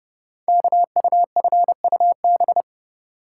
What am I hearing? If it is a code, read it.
Morse code, heard as a keyed tone, about 30 wpm, 700 Hz. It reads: KUFUB
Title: Code Groups - 5 Letters - 30wpm